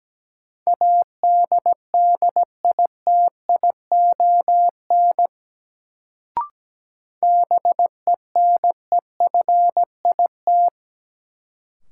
Morse code, heard as a keyed tone, about 17 wpm, 700 Hz. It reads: ADDITION BENEFIT
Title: Sentences from Top 1000 Words - Review Part 6 - New Word Drill - 17wpm